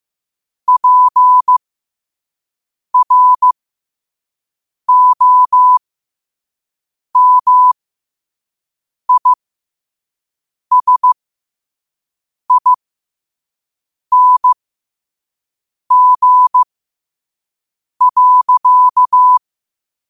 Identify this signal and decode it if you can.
Morse code, heard as a keyed tone, about 15 wpm, spaced as for 5 wpm, 1000 Hz. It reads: PROMISING.
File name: qotdfiles / 5wpm